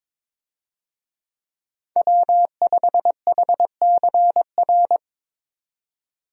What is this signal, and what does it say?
Morse code, heard as a keyed tone, about 22 wpm, 700 Hz. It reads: W5HCR
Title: USA Call Signs - 5 Characters - 22wpm